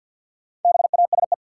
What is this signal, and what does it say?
Morse code, heard as a keyed tone, about 52 wpm, 700 Hz. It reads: BASE
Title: Sentences from Top 1000 Words - Review Part 1 - New Word Drill - 50wpm